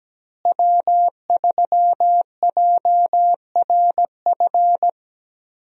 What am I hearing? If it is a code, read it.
Morse code, heard as a keyed tone, about 17 wpm, 700 Hz. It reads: W3JRF
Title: USA Call Signs - 5 Characters - Repeated 2x - 17wpm